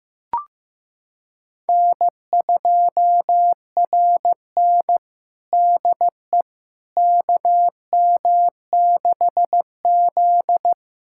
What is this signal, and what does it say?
Morse code, heard as a keyed tone, about 15 wpm, 700 Hz. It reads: N2RN DE KM6Z
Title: USA Callsign DE Callsign - 4 Characters - 15wpm